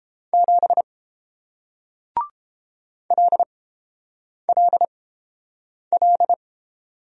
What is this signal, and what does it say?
Morse code, heard as a keyed tone, about 30 wpm, spaced as for 7 wpm, 700 Hz. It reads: 7 LLL